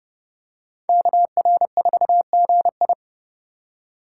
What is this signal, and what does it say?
Morse code, heard as a keyed tone, about 29 wpm, 700 Hz. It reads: KR4GI